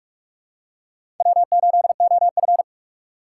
Morse code, heard as a keyed tone, about 43 wpm, 700 Hz. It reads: W8OF
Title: USA Call Signs - 4 Characters - 6x Character Spacing - 45wpm